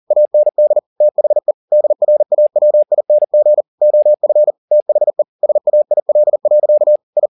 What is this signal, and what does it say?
Morse code, heard as a keyed tone, about 40 wpm, 600 Hz. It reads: AND THE DRAWING OF THE SAIL. I
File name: the old man and the sea 2 40wpm